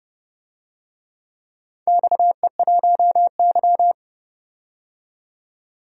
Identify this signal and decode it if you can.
Morse code, heard as a keyed tone, about 30 wpm, 700 Hz. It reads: XE1Y
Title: DXCC-100 Call Signs - 4 Characters - 30wpm